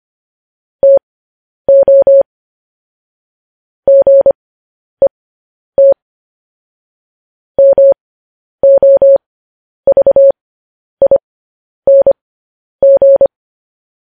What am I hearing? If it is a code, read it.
Morse code, heard as a keyed tone, about 25 wpm, spaced as for 10 wpm, 550 Hz. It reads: TO GET MOVING
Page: news headlines in morse code at 10 wpm